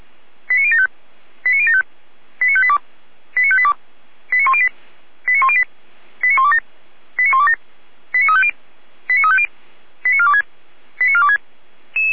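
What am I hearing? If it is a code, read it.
Five-tone selective call (ZVEI), heard as quick runs of five tones, tones of about 70 ms. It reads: 89075 89075 89582 89582 89108 89108 89127 89127 89350 89350 89426 89426
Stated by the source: ZVEI five-tone selective call 89013 received on an SDR.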